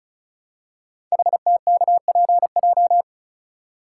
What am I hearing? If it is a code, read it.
Morse code, heard as a keyed tone, about 35 wpm, 700 Hz. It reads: HTKPJ
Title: Code Groups - 5 Letter:Numbers - 35wpm